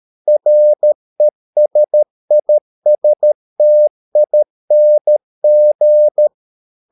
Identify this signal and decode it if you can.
Morse code, heard as a keyed tone, about 13 wpm, 600 Hz. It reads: RESISTING